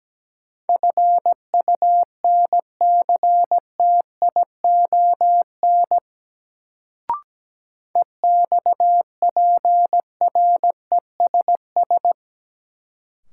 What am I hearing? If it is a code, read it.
Morse code, heard as a keyed tone, about 17 wpm, 700 Hz. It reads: FUNCTION EXPRESS
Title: Sentences from Top 1100 Words - New Word Drill - 17wpm